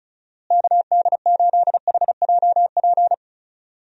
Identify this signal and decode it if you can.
Morse code, heard as a keyed tone, about 36 wpm, 700 Hz. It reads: KD8HJP
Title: USA Call Signs - 6 Characters - Repeated 2x - 35wpm